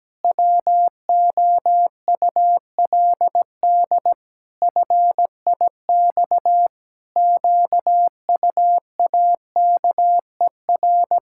Morse code, heard as a keyed tone, about 17 wpm, 700 Hz. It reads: WOULD FIX QUAKER
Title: Pangram Challenge v2 - 17wpm